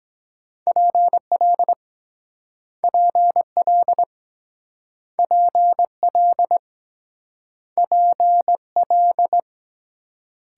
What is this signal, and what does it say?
Morse code, heard as a keyed tone, about 23 wpm, 700 Hz. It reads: PL PL PL PL